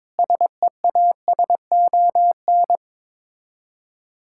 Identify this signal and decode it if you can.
Morse code, heard as a keyed tone, about 22 wpm, 700 Hz. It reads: SEASON
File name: practice / 22wpm / Sets of 7 Words - 22wpm